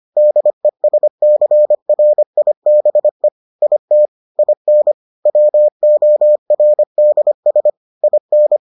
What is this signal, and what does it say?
Morse code, heard as a keyed tone, about 25 wpm, 600 Hz. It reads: DESCRIBE IT IN WORDS IN